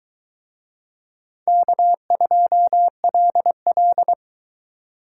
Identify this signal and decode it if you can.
Morse code, heard as a keyed tone, about 22 wpm, 700 Hz. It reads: K2LL